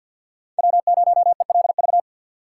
Morse code, heard as a keyed tone, about 48 wpm, 700 Hz. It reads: W0ELV
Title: Code Groups - 5 Letter:Numbers - 50wpm